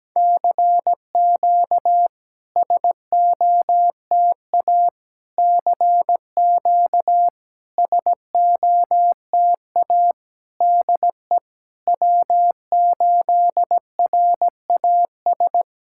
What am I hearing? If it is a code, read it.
Morse code, heard as a keyed tone, about 17 wpm, 700 Hz. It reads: CQ SOTA CQ SOTA DE W8RAS
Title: SOTA QSO Practice 17wpm